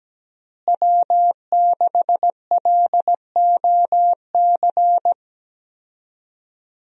Morse code, heard as a keyed tone, about 17 wpm, 700 Hz. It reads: W6LOC
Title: USA Call Signs - 5 Characters - Repeated 2x - 17wpm